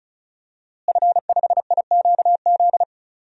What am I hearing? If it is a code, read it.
Morse code, heard as a keyed tone, about 34 wpm, 700 Hz. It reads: F5IQZ
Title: DXCC-100 Call Signs - 5 Characters - Repeated 2x - 35wpm